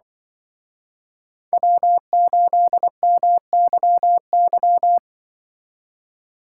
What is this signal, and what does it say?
Morse code, heard as a keyed tone, about 25 wpm, 700 Hz. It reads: W8MYY
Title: Speed Racing - USA 1x3 Call Signs 20wpm